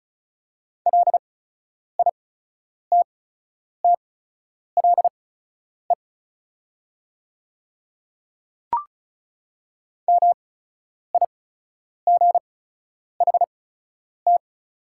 Morse code, heard as a keyed tone, about 34 wpm, spaced as for 10 wpm, 700 Hz. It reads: LITTLE MIGHT
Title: 8x Character Spacing - Top 300 Words - 35wpm